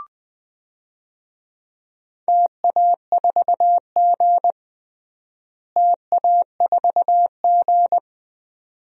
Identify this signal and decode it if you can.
Morse code, heard as a keyed tone, about 19 wpm, 700 Hz. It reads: TA4G TA4G
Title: DXCC-100 Call Signs - 4 Characters - Repeated 2x - 20wpm